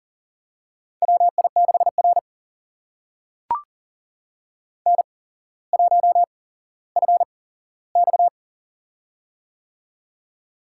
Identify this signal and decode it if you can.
Morse code, heard as a keyed tone, about 40 wpm, spaced as for 17 wpm, 700 Hz. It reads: WI6R N1FX
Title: USA Call Signs - 4 Characters - 8x Character Spacing - 40wpm